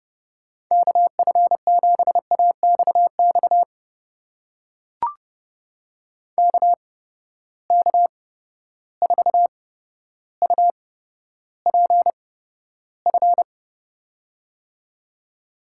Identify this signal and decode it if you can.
Morse code, heard as a keyed tone, about 32 wpm, spaced as for 13 wpm, 700 Hz. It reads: KF7AXX KK4UPF
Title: USA Call Signs - 6 Characters - 8x Character Spacing - 30wpm